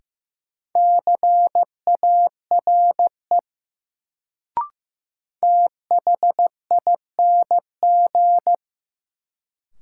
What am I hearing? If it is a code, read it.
Morse code, heard as a keyed tone, about 15 wpm, 700 Hz. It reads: CARE THING
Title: Sets of 3 Words - New Word Drill - 15wpm